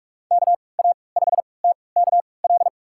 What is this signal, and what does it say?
Morse code, heard as a keyed tone, about 46 wpm, spaced as for 26 wpm, 700 Hz. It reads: KA5TKL
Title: USA Call Signs - 6 Characters - 3x Character Spacing - 45wpm